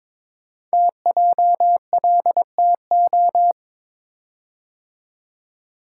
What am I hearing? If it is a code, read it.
Morse code, heard as a keyed tone, about 22 wpm, 700 Hz. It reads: TJLTO
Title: Code Groups - 5 Letters - 22wpm